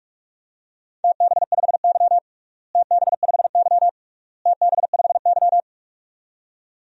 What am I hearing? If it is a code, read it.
Morse code, heard as a keyed tone, about 46 wpm, 700 Hz. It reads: TB5Y TB5Y TB5Y
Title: DXCC-100 Call Signs - 4 Characters - Repeated 3x - 45wpm